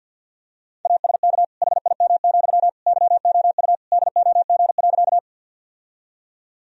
Morse code, heard as a keyed tone, about 50 wpm, 700 Hz. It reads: ASK HIM, YOU DOG.
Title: Sentences from Top 1000 Words - Review Part 2 - 50wpm